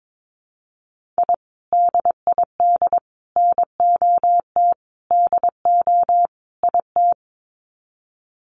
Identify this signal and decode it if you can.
Morse code, heard as a keyed tone, about 22 wpm, 700 Hz. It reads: I DID NOT DO IT
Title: Sets of 5 Words Encore2 22wpm